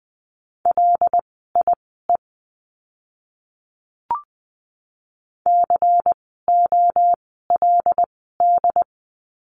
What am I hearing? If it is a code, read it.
Morse code, heard as a keyed tone, about 20 wpm, spaced as for 14 wpm, 700 Hz. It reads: LIE COLD
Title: 094 Lesson 16f - Letter c - Words 20wpm